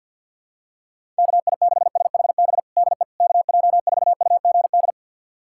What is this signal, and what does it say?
Morse code, heard as a keyed tone, about 52 wpm, 700 Hz. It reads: KI6SHB DE KJ4UGD
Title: USA Callsign DE Callsign 50wpm